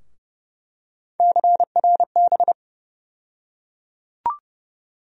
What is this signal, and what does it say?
Morse code, heard as a keyed tone, about 30 wpm, 700 Hz. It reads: CRB
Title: USA Call Sign Postfixes - 3 Letters 30wpm